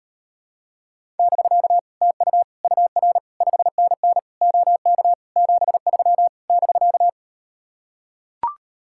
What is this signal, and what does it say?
Morse code, heard as a keyed tone, about 38 wpm, 700 Hz. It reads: <BK> TU UR 5NN OK 73 <BK>